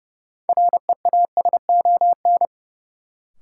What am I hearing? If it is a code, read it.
Morse code, heard as a keyed tone, about 29 wpm, 700 Hz. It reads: REASON
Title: Sentences from Top 700 Words - New Word Drill - 30wpm